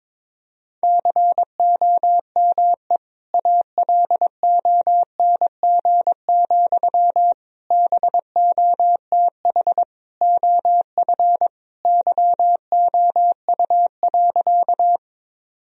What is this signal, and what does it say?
Morse code, heard as a keyed tone, about 22 wpm, 700 Hz. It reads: COME ALONG, BOTH OF YOU.
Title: Sentences from Top 1000 Words - Review Part 1 - 22wpm